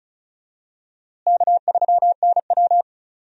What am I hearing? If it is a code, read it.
Morse code, heard as a keyed tone, about 34 wpm, 700 Hz. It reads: K3NW
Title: USA Call Signs - 4 Characters - Repeated 2x - 35wpm